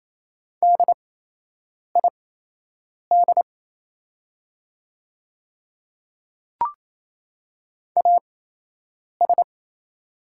Morse code, heard as a keyed tone, about 28 wpm, spaced as for 8 wpm, 700 Hz. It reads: DID AS
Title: 8x Character Spacing - Top 100 Words - 28wpm